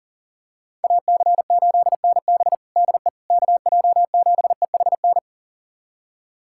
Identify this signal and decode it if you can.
Morse code, heard as a keyed tone, about 40 wpm, 700 Hz. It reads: AC8NB DE KJ7EHN
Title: USA Callsign DE Callsign 40wpm